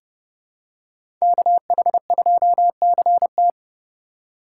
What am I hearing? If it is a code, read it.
Morse code, heard as a keyed tone, about 30 wpm, 700 Hz. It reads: KH2CT
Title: USA Call Signs - 4x Character Spacing - 30wpm